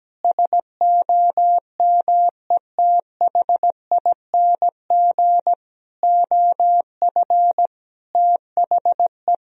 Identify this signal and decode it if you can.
Morse code, heard as a keyed tone, about 17 wpm, 700 Hz. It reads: SOMETHING OF THE
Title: Sentences from Top 1000 Words - Review Part 3 - 17wpm